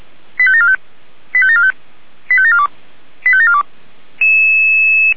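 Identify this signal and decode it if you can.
Five-tone selective call (ZVEI), heard as quick runs of five tones, tones of about 70 ms. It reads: 85745 85745 85742 85742
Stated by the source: ZVEI five-tone selective call received on an SDR.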